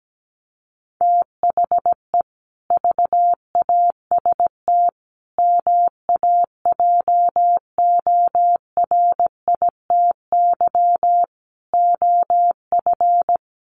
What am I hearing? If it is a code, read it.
Morse code, heard as a keyed tone, about 17 wpm, 700 Hz. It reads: THE VAST MAJORITY OF
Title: Sets of 4 Words 17wpm